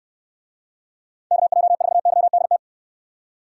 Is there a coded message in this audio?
Yes: Morse code from 1.3 to 2.6 s, heard as a keyed tone, about 69 wpm, 700 Hz. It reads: KJ4YDT